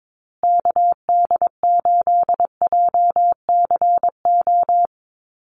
Morse code, heard as a keyed tone, about 22 wpm, 700 Hz. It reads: KD8JCO
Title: USA Call Signs Repeated 3x 22wpm